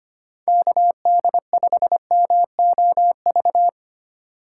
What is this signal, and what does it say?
Morse code, heard as a keyed tone, about 25 wpm, 700 Hz. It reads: KD5MOV
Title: USA Call Signs - 6 Characters - 3x Character Spacing - 25wpm